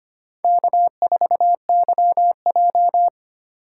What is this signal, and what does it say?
Morse code, heard as a keyed tone, about 25 wpm, 700 Hz. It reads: K4YJ
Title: USA Call Signs - 4 Characters - 4x Character Spacing - 25wpm